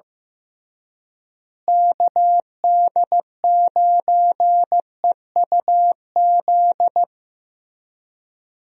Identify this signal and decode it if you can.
Morse code, heard as a keyed tone, about 15 wpm, 700 Hz. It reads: KD9EUZ
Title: USA Call Signs - 6 Characters - Repeated 3x - 15wpm